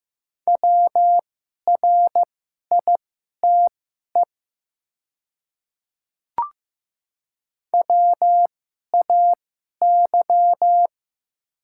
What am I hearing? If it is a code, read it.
Morse code, heard as a keyed tone, about 15 wpm, spaced as for 11 wpm, 700 Hz. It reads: WRITE WAY
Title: 2x Character Spacing - Top 100 Words - 15wpm